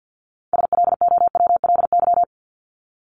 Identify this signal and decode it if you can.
Morse code, heard as a keyed tone, about 50 wpm, 700 Hz. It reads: SLOWLY
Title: Top 1001-1250 Words 50wpm